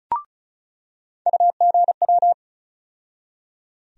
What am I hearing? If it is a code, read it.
Morse code, heard as a keyed tone, about 36 wpm, 700 Hz. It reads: UGW
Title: USA Call Sign Postfixes - 3 Letters 35wpm